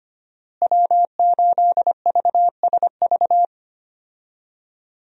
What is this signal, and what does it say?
Morse code, heard as a keyed tone, about 25 wpm, 700 Hz. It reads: W8VSV
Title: USA Call Signs - 5 Characters - Repeated 2x - 25wpm